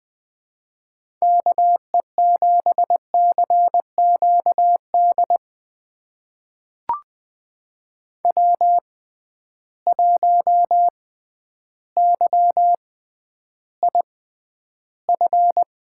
Morse code, heard as a keyed tone, about 20 wpm, spaced as for 11 wpm, 700 Hz. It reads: KE7CQD W1YIF
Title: USA Call Signs - 6x Character Spacing - 20wpm